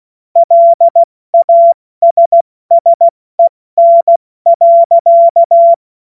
Morse code, heard as a keyed tone, about 16 wpm, spaced as for 14 wpm, 680 Hz. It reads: LASSEN.
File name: Morsekurs / Witze / Witz 02. Mai 2023 70 F80